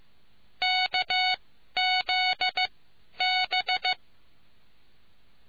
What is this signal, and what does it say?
Morse code, heard as a keyed tone, about 15 wpm, spaced as for 11 wpm, 740 Hz. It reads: KZB